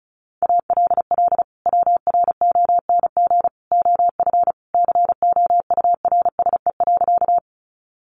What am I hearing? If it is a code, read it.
Morse code, heard as a keyed tone, about 34 wpm, 700 Hz. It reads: ALL WRONG OF COURSE.
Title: Sentences from Top 900 Words 35wpm